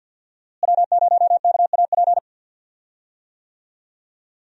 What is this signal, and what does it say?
Morse code, heard as a keyed tone, about 47 wpm, 700 Hz. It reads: W0KAP
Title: USA Call Signs - 5 Characters - Repeated 3x - 50wpm